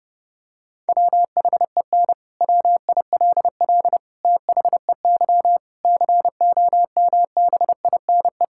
Morse code, heard as a keyed tone, about 30 wpm, 700 Hz. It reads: WHEN WILL THEY COMBINE
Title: Sentences from Top 1600 Words 30wpm